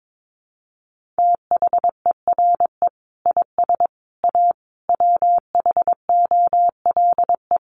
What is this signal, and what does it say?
Morse code, heard as a keyed tone, about 22 wpm, 700 Hz. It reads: THERE IS A WHOLE
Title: Sentences from Top 1400 Words 22wpm